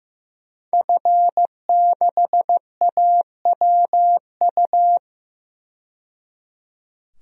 Morse code, heard as a keyed tone, about 15 wpm, 700 Hz. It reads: F6AWU
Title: DXCC-100 Call Signs - 5 Characters - 15wpm